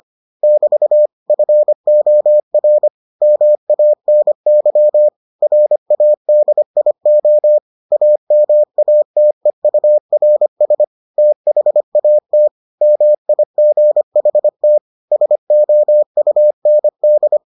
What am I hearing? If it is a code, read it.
Morse code, heard as a keyed tone, about 25 wpm, 600 Hz. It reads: <BT> FOR MANY RADIO AMATEURS THAT MIGHT SOUND